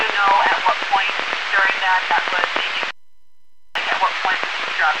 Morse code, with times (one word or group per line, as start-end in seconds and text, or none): none